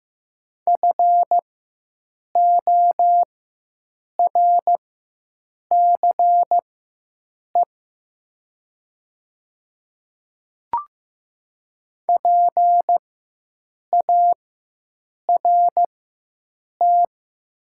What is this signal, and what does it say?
Morse code, heard as a keyed tone, about 15 wpm, spaced as for 7 wpm, 700 Hz. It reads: FORCE PART
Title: 4x Character Spacing - Top 500 Words - 15wpm